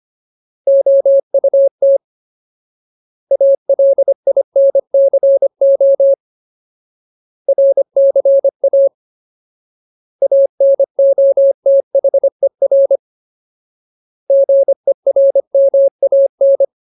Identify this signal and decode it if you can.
Morse code, heard as a keyed tone, about 25 wpm, 550 Hz. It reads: OUT ALINCO RCA ANOTHER GERMAN
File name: Ultimate Word List / 25WPM Ultimate